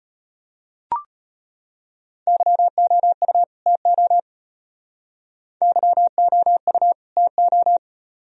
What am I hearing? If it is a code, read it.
Morse code, heard as a keyed tone, about 33 wpm, 700 Hz. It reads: YOU TO YOU TO